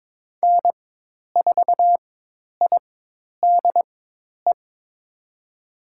Morse code, heard as a keyed tone, about 22 wpm, spaced as for 10 wpm, 700 Hz. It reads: N4IDE
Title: USA Call Signs - 5 Characters - 4x Character Spacing - 22wpm